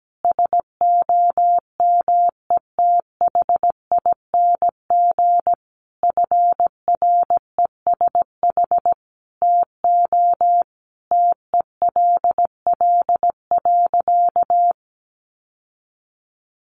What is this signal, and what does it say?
Morse code, heard as a keyed tone, about 17 wpm, 700 Hz. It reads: SOMETHING FRESH TO TELL.
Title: Sentences from Top 1200 Words 17wpm